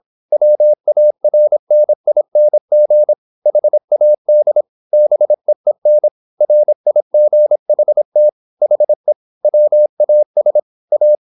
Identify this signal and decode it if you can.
Morse code, heard as a keyed tone, about 26 wpm, 600 Hz. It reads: WARNING HAD BEEN RIGHT HE WAS A
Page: Never 26wpm | CW med Gnister